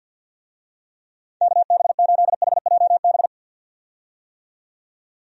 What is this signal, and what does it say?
Morse code, heard as a keyed tone, about 46 wpm, 700 Hz. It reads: KB8HJB
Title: USA Call Signs - 6 Characters - 50wpm